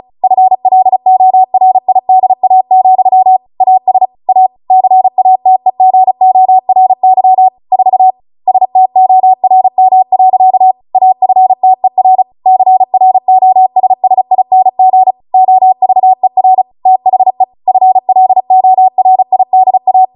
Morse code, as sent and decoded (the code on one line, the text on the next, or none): ..-. .-.. --- .-. .. -.. .- --..-- / .- ... / .- / -.-. .- - . --. --- .-. -.-- / ....- / ... - --- .-. -- .-.-.- / .- ..-. - . .-. / -.-. .-. --- ... ... .. -. --. / --- ...- . .-. / - .... / ..-. .-.. --- .-. .. -.. .-
FLORIDA, AS A CATEGORY 4 STORM. AFTER CROSSING OVER THE FLORIDA